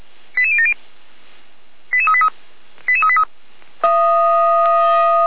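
Five-tone selective call (ZVEI), heard as quick runs of five tones, tones of about 70 ms. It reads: 88070 88383 88383
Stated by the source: ZVEI five-tone selective call received by SDR.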